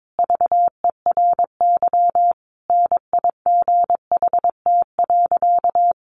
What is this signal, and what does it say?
Morse code, heard as a keyed tone, about 22 wpm, 700 Hz. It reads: VERY NIGHT.
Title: Sentences from Top 1200 Words 22wpm